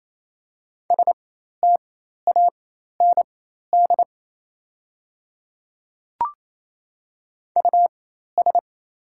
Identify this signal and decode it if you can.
Morse code, heard as a keyed tone, about 28 wpm, spaced as for 13 wpm, 700 Hz. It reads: STAND US
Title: 4x Character Spacing - Top 500 Words - 28wpm